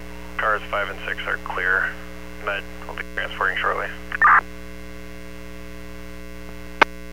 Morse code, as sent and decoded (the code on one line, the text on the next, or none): none